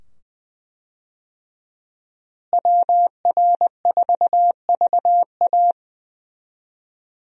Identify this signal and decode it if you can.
Morse code, heard as a keyed tone, about 20 wpm, 700 Hz. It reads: WR4VA
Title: USA Call Signs - 5 Characters - Repeated 2x - 20wpm